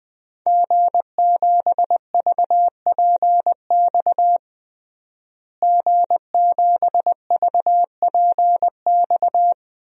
Morse code, heard as a keyed tone, about 20 wpm, 700 Hz. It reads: G7VPX G7VPX
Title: DXCC-100 Call Signs - 5 Characters - Repeated 3x - 20wpm